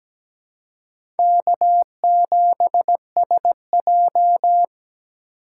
Morse code, heard as a keyed tone, about 17 wpm, 700 Hz. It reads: K7SJ
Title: USA Call Signs - 4 Characters - 4x Character Spacing - 17wpm